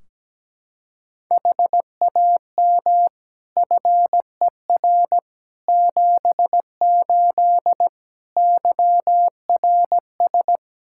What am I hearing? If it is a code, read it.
Morse code, heard as a keyed tone, about 17 wpm, 700 Hz. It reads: HAM FER 78 YRS
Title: Practice QSO - Exchange 3 - More Info - 17wpm